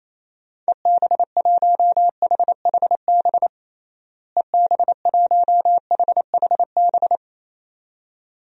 Morse code, heard as a keyed tone, about 28 wpm, 700 Hz. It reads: EB1HHB EB1HHB
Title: DXCC-100 Call Signs - 6 Characters - Repeated 2x - 28wpm